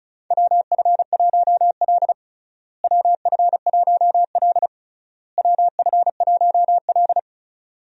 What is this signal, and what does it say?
Morse code, heard as a keyed tone, about 36 wpm, 700 Hz. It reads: WF1L WF1L WF1L
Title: USA Call Signs - 4 Characters - Repeated 3x - 35wpm